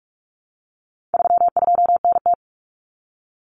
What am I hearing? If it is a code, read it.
Morse code, heard as a keyed tone, about 44 wpm, 700 Hz. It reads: 32NT
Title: Common Cut Numbers - 4-Digit 45wpm